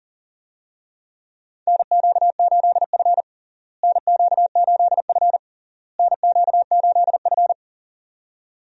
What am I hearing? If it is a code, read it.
Morse code, heard as a keyed tone, about 37 wpm, 700 Hz. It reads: NQ8F NQ8F NQ8F
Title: USA Call Signs - 4 Characters - Repeated 3x - 40wpm